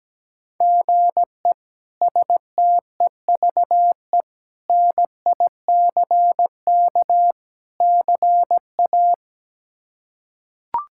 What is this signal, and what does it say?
Morse code, heard as a keyed tone, about 17 wpm, 700 Hz. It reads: GE STEVE NICK CA
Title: K1USN SST QSO Practice - 17wpm